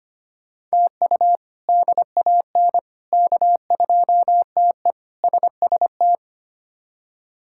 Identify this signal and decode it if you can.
Morse code, heard as a keyed tone, about 25 wpm, 700 Hz. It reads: TU DAN K2TE SST
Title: K1USN SST QSO Practice - 25wpm